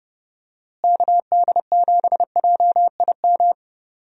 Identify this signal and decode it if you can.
Morse code, heard as a keyed tone, about 30 wpm, 700 Hz. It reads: KD7JIM
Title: USA Call Signs - 6 Characters - Repeated 3x - 30wpm